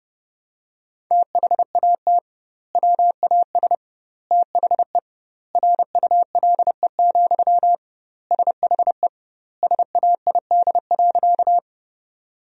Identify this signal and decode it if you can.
Morse code, heard as a keyed tone, about 30 wpm, 700 Hz. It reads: THAT WAS THE RULE, SHE SAID.
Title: 2x Word-Spacing - Sentences from Top 400 Words - 30wpm